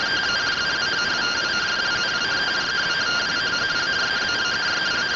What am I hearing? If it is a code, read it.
Two-tone teleprinter signal, about 170 Hz shift.